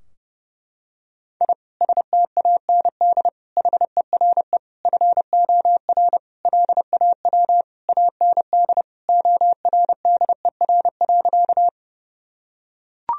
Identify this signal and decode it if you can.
Morse code, heard as a keyed tone, about 30 wpm, 700 Hz. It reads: I STAND HERE FOR LAW AND ORDER.
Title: Sentences from Top 1000 Words - Review Part 3 - 30wpm